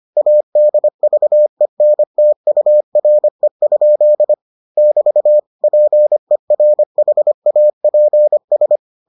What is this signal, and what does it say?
Morse code, heard as a keyed tone, about 25 wpm, 600 Hz. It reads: ADVENTURE? <BT> PERHAPS